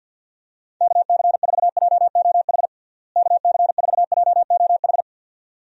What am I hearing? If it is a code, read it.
Morse code, heard as a keyed tone, about 52 wpm, 700 Hz. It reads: KC4JOH KC4JOH
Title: USA Call Signs - 6 Characters - Repeated 2x - 50wpm